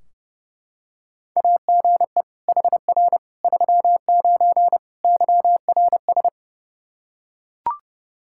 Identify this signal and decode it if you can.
Morse code, heard as a keyed tone, about 32 wpm, 700 Hz. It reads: AGE HR 39 YRS
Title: Practice QSO - Exchange 3 - More Info - 30wpm